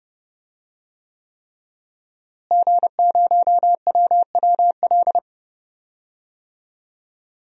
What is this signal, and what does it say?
Morse code, heard as a keyed tone, about 32 wpm, 700 Hz. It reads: G0WWL
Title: DXCC-100 Call Signs - 5 Characters - Repeated 3x - 30wpm